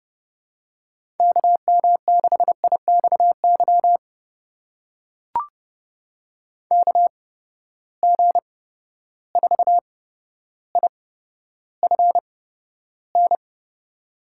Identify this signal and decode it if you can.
Morse code, heard as a keyed tone, about 30 wpm, spaced as for 13 wpm, 700 Hz. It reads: KM6IXY KG4IFN
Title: USA Call Signs - 6 Characters - 8x Character Spacing - 30wpm